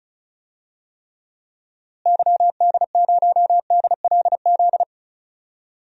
Morse code, heard as a keyed tone, about 36 wpm, 700 Hz. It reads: YD0DLZ
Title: DXCC-100 Call Signs - 6 Characters - Repeated 3x - 35wpm